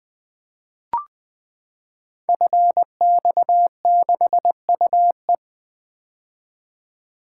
Morse code, heard as a keyed tone, about 20 wpm, 700 Hz. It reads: FX6UE
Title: Code Groups - 5 Letter:Numbers - 20wpm